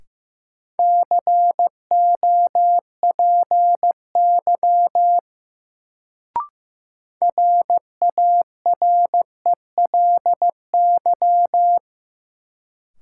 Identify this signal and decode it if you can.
Morse code, heard as a keyed tone, about 15 wpm, 700 Hz. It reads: COPY RARELY
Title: Sentences from Top 1400 Words - New Word Drill - 15wpm